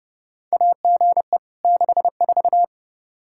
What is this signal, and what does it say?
Morse code, heard as a keyed tone, about 30 wpm, 700 Hz. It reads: AGE 64
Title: Practice QSO - Exchange 3 - More Info - 30wpm